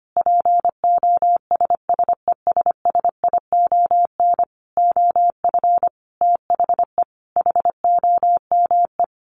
Morse code, heard as a keyed tone, about 25 wpm, 700 Hz. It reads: POSSESSION OF THE HOME
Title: Sentences from Top 1400 Words 25wpm